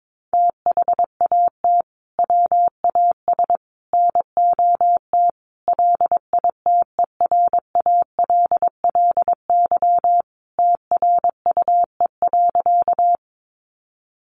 Morse code, heard as a keyed tone, about 22 wpm, 700 Hz. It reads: THAT WAS NOT LITERALLY TRUE.
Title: Sentences from Top 1400 Words 22wpm